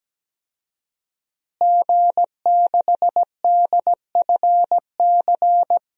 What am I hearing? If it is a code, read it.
Morse code, heard as a keyed tone, about 17 wpm, 700 Hz. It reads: G6DFC
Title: DXCC-100 Call Signs - 5 Characters - Repeated 3x - 17wpm